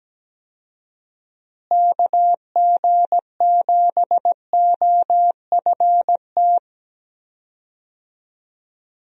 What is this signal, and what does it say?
Morse code, heard as a keyed tone, about 17 wpm, 700 Hz. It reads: KG7OFT
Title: USA Call Signs - 6 Characters - Repeated 3x - 17wpm